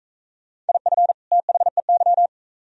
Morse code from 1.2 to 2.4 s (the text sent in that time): THEY